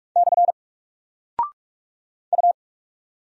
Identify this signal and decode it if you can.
Morse code, heard as a keyed tone, about 48 wpm, 700 Hz. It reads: / U
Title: Single Character (Letter:Number:Punctuation:Prosign) Alt 45wpm